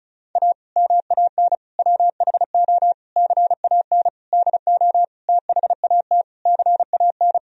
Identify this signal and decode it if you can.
Morse code, heard as a keyed tone, about 34 wpm, 700 Hz. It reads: A MAN WHO CAN DO THAT CAN
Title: Sentences from Top 1000 Words - Review Part 2 - 35wpm